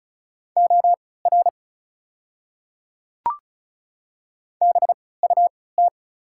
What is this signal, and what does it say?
Morse code, heard as a keyed tone, about 36 wpm, spaced as for 20 wpm, 690 Hz. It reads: OR BUT
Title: 3x Character Spacing - Top 100 Words - 35wpm